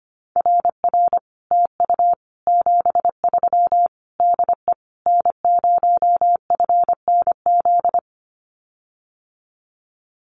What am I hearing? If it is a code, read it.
Morse code, heard as a keyed tone, about 25 wpm, 700 Hz. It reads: RR TU 73 DE N0FNZ
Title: Common QSO Phrases Encore 25wpm